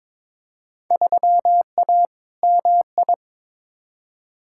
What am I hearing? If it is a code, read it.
Morse code, heard as a keyed tone, about 22 wpm, 700 Hz. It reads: 3A MI
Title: Field Day Exchanges - 22wpm